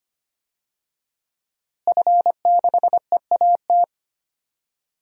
Morse code, heard as a keyed tone, about 25 wpm, 700 Hz. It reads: F6EAT
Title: DXCC-100 Call Signs - 5 Characters - Repeated 3x - 25wpm